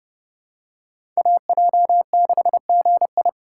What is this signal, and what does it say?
Morse code, heard as a keyed tone, about 30 wpm, 700 Hz. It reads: AJ6GI
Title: USA Call Signs - 5 Characters - 4x Character Spacing - 30wpm